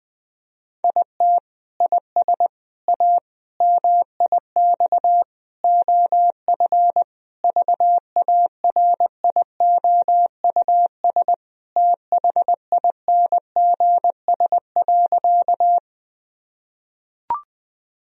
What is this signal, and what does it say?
Morse code, heard as a keyed tone, about 21 wpm, 700 Hz. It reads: IT IS A MIX OF VARIOUS THINGS.
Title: Sentences from Top 1000 Words - Review Part 4 - 20wpm